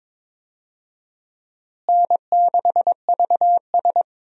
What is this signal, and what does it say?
Morse code, heard as a keyed tone, about 22 wpm, 700 Hz. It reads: N6VS
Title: USA Call Signs - 4 Characters - Repeated 3x - 22wpm